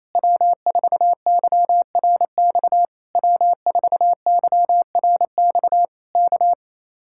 Morse code, heard as a keyed tone, about 26 wpm, 700 Hz. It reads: W4YRX W4YRX K
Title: Speed-Racing Common QSO Phrases - 28wpm